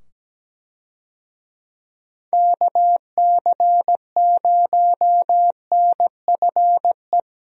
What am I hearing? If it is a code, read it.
Morse code, heard as a keyed tone, about 17 wpm, 700 Hz. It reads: KC0NFE
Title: USA Call Signs - 6 Characters - 17wpm